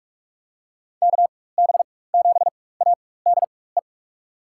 Morse code, heard as a keyed tone, about 44 wpm, spaced as for 21 wpm, 700 Hz. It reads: KB7ADE
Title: USA Call Signs - 4x Character Spacing - 45wpm